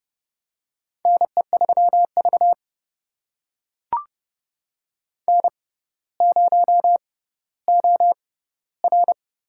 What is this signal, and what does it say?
Morse code, heard as a keyed tone, about 31 wpm, spaced as for 15 wpm, 700 Hz. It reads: NE3V N0OR
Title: USA Call Signs - 4 Characters - 6x Character Spacing - 30wpm